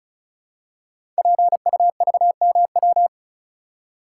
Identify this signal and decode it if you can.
Morse code, heard as a keyed tone, about 36 wpm, 700 Hz. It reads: PUVMW
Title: Code Groups - 5 Letters - 35wpm